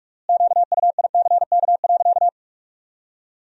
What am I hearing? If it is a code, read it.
Morse code, heard as a keyed tone, about 42 wpm, 700 Hz. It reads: QUICK.
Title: 4x Word-Spacing - Sentences from Top 500 Words - 45wpm